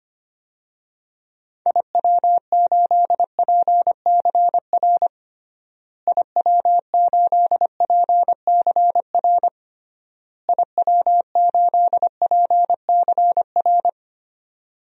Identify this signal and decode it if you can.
Morse code, heard as a keyed tone, about 25 wpm, 700 Hz. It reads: IW8PCR IW8PCR IW8PCR
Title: DXCC-100 Call Signs - 6 Characters - Repeated 3x - 25wpm